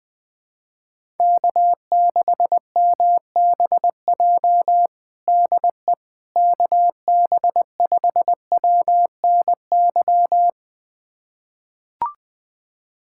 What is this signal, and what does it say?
Morse code, heard as a keyed tone, about 20 wpm, 700 Hz. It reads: K6MBJ DE KB5WNY
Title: USA Callsign DE Callsign 20wpm